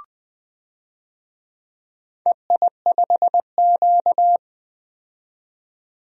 Morse code, heard as a keyed tone, about 20 wpm, 700 Hz. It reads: EI5Q
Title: DXCC-100 Call Signs - 4 Characters - 20wpm